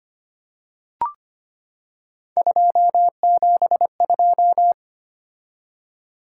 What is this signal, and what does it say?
Morse code, heard as a keyed tone, about 25 wpm, 700 Hz. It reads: 272
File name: Three Letter-Number Combinations - 25wpm